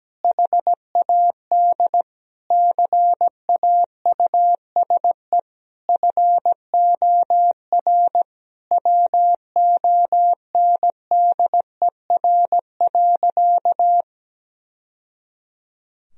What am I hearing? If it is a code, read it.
Morse code, heard as a keyed tone, about 17 wpm, 700 Hz. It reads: HAD CAUSE FOR WONDER.
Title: Sentences from Top 1000 Words - Review Part 3 - 17wpm